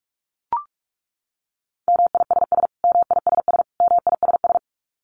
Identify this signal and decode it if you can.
Morse code, heard as a keyed tone, about 44 wpm, 700 Hz. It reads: MISS MISS MISS
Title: Top 500 Words Repeated 45wpm